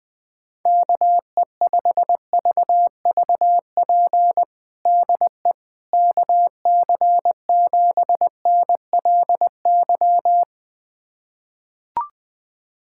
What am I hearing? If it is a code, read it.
Morse code, heard as a keyed tone, about 20 wpm, 700 Hz. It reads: KE5VVP DE KC7NLY
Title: USA Callsign DE Callsign 20wpm